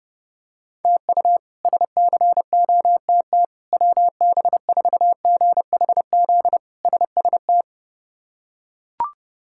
Morse code, heard as a keyed tone, about 31 wpm, 700 Hz. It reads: TU SCOTT WB4GHZ SST
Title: K1USN SST QSO Practice - 30wpm